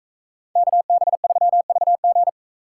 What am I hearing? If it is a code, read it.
Morse code, heard as a keyed tone, about 41 wpm, 700 Hz. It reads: KB3VG